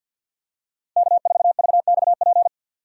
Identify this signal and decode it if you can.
Morse code, heard as a keyed tone, about 53 wpm, 700 Hz. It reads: K4VXP
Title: USA Call Signs - 5 Characters - 4x Character Spacing - 50wpm